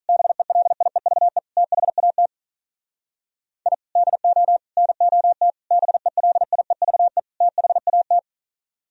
Morse code, heard as a keyed tone, about 42 wpm, 700 Hz. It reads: BELIEVE THAT I DO NOT BELIEVE THAT
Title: Speed-Racing - Sets of 5 Words 35wpm